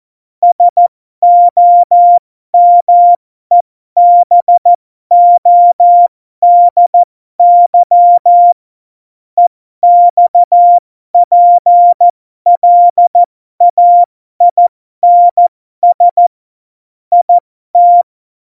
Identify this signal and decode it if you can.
Morse code, heard as a keyed tone, about 14 wpm, spaced as for 12 wpm, 700 Hz. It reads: SOMEBODY EXPLAINS IT
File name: Morsekurs / Witze